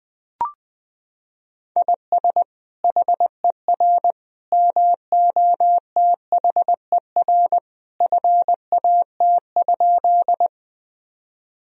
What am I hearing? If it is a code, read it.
Morse code, heard as a keyed tone, about 21 wpm, 700 Hz. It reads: IS HER MOTHER FAT?
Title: Sentences from Top 1000 Words - Review Part 4 - 20wpm